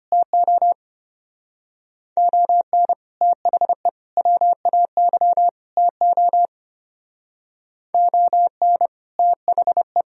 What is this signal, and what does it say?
Morse code, heard as a keyed tone, about 30 wpm, 700 Hz. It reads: TO ON THE WAY TO ON THE
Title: Speed-Racing - Sets of 4 Words 25wpm